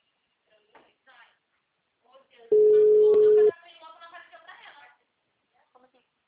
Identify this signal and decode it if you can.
Ringback tone (Europe, 425 Hz), with speech.